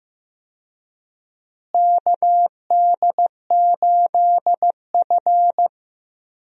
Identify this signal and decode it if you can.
Morse code, heard as a keyed tone, about 15 wpm, 700 Hz. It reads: KD8F
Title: USA Call Signs - 4 Characters - Repeated 2x - 15wpm